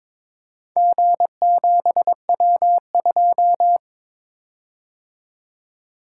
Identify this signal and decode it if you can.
Morse code, heard as a keyed tone, about 22 wpm, 700 Hz. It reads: G7W2
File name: practice / 22wpm / Four Letter-Number Combinations - 22wpm